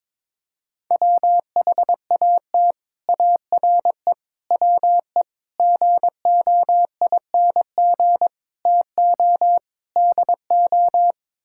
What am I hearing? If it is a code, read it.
Morse code, heard as a keyed tone, about 22 wpm, 700 Hz. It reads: WHAT ARE WE GOING TO DO
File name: practice / 22wpm / Sets of 6 Words - 22wpm